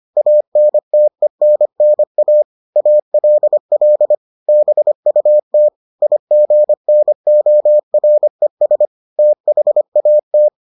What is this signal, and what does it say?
Morse code, heard as a keyed tone, about 25 wpm, 600 Hz. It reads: ANTENNA ALL BUT IGNORES THAT